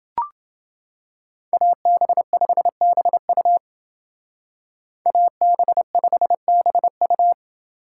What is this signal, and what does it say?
Morse code, heard as a keyed tone, about 29 wpm, 700 Hz. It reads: AB5BU AB5BU